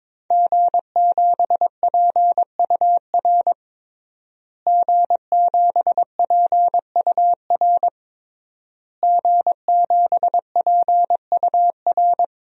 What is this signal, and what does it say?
Morse code, heard as a keyed tone, about 22 wpm, 700 Hz. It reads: G7PUR G7PUR G7PUR
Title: DXCC-100 Call Signs - 5 Characters - Repeated 3x - 22wpm